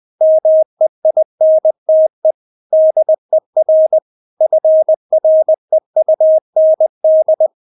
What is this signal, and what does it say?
Morse code, heard as a keyed tone, about 20 wpm, 630 Hz. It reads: MEINTE DER FREUND
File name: Witz1 20wpm